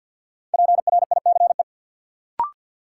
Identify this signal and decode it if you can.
Morse code, heard as a keyed tone, about 46 wpm, 700 Hz. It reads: PRICE
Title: Sentences from Top 600 Words - New Word Drill - 50wpm